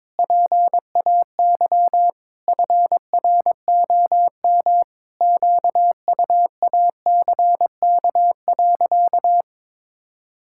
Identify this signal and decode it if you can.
Morse code, heard as a keyed tone, about 22 wpm, 700 Hz. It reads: PAY FROM QUACK.
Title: Pangram Challenge v2 - 22wpm